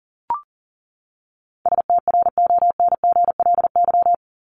Morse code, heard as a keyed tone, about 40 wpm, 700 Hz. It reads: STRONGLY
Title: LY Suffix - Top 100 Words with Base Words 40wpm